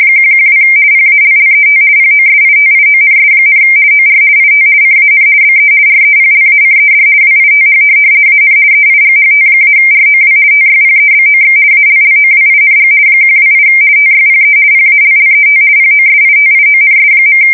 RTTY, about 50 baud. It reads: 112 PALLET POSITIONS, OR IT CAN BE REPLACED WITH AN EIGHT-ROW TYPEBOX ALLOWING 128 TYPE PALLET POSITIONS.
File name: fsk170-50B / teletype-corp